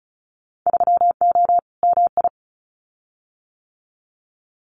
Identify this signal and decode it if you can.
Morse code, heard as a keyed tone, about 36 wpm, 700 Hz. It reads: 3O MI
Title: Winter Field Day Exchanges Encore 35wpm